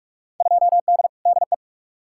Morse code, heard as a keyed tone, about 42 wpm, 700 Hz. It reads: 2D DE
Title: Field Day Exchanges Encore - 45wpm